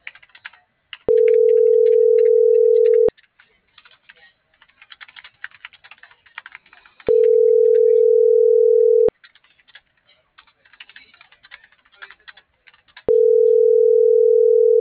Ringback tone (North America), 440 + 480 Hz.